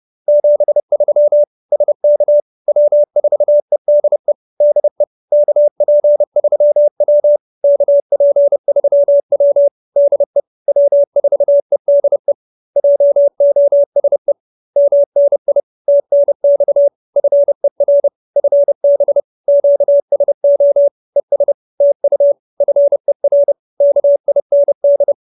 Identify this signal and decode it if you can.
Morse code, heard as a keyed tone, about 30 wpm, 570 Hz. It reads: 73 SK W4EDE DE KP3W KP3W DE W4EDE JOSE MNI TNX FER FB QSO ES TU FER KIND
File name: QSO317 30